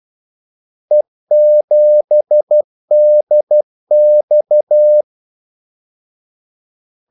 Morse code, heard as a keyed tone, about 12 wpm, 600 Hz. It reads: E7DX
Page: 2 ekstra ordmellemrum mellem kaldesignalerne